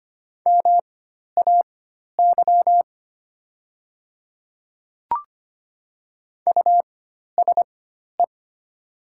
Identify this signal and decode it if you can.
Morse code, heard as a keyed tone, about 25 wpm, spaced as for 12 wpm, 700 Hz. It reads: MAY USE
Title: 4x Character Spacing - Top 300 Words - 25wpm